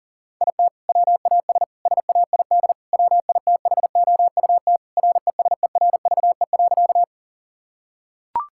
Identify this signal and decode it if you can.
Morse code, heard as a keyed tone, about 40 wpm, 700 Hz. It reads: IT WAS SAID WITHOUT RESERVE.
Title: Sentences from Top 1000 Words - Review Part 6 - 40wpm